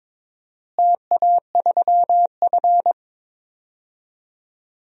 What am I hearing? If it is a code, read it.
Morse code, heard as a keyed tone, about 21 wpm, 700 Hz. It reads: TA3F
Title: DXCC-100 Call Signs - 4 Characters - Repeated 2x - 22wpm